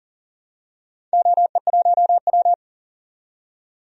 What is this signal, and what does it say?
Morse code, heard as a keyed tone, about 40 wpm, 700 Hz. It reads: OE1W
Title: DXCC-100 Call Signs - 4 Characters - 40wpm